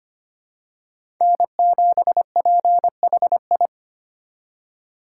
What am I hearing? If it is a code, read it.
Morse code, heard as a keyed tone, about 25 wpm, 700 Hz. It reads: N7PHI
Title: K1USN SST QSO Practice - 25wpm